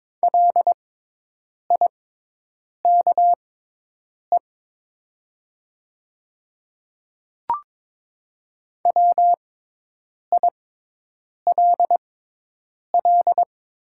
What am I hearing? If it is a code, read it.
Morse code, heard as a keyed tone, about 22 wpm, spaced as for 8 wpm, 700 Hz. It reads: LIKE WILL
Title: 6x Character Spacing - Top 100 Words - 22wpm